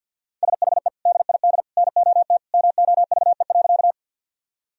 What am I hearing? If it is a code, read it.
Morse code, heard as a keyed tone, about 53 wpm, 700 Hz. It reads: SHE DID NOT MOVE.
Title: Sentences from Top 1000 Words - Review Part 1 - 50wpm